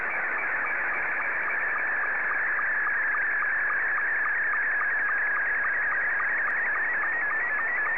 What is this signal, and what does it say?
SSTV (slow-scan television), Scottie 2.
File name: sstv / S2-JE1YEM